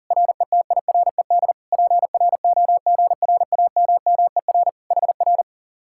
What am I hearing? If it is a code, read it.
Morse code, heard as a keyed tone, about 50 wpm, 700 Hz. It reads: RETIRED PROGRAMMER HR